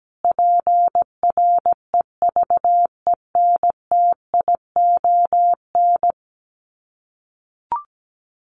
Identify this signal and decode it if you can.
Morse code, heard as a keyed tone, about 17 wpm, 700 Hz. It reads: PREVENTION
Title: Sets of 5 Words Encore2 17wpm